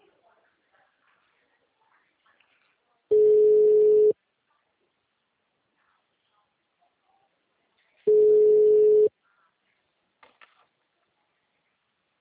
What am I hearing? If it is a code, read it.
Ringback tone (Europe, 425 Hz).
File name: out- -2002- - - .1082